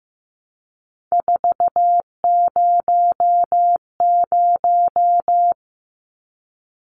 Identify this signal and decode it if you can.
Morse code, heard as a keyed tone, about 15 wpm, 700 Hz. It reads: 400
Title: Numbers - 3-Digit 15wpm